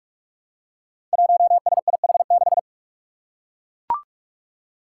Morse code, heard as a keyed tone, about 42 wpm, 700 Hz. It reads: 1SIH6
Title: Five Challenging Letter-Number Combinations 45wpm